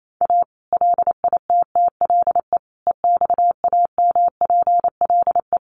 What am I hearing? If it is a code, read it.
Morse code, heard as a keyed tone, about 28 wpm, 700 Hz. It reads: A LITTLE EXAMPLE
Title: Sentences from Top 1500 Words 28wpm